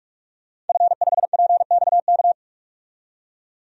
Morse code, heard as a keyed tone, about 46 wpm, 700 Hz. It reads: F5PXK
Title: DXCC-100 Call Signs - 5 Characters - 45wpm